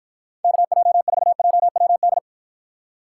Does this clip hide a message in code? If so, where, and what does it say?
Yes: Morse code from 0.4 to 2.2 s, heard as a keyed tone, about 52 wpm, 700 Hz. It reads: KJ4JWD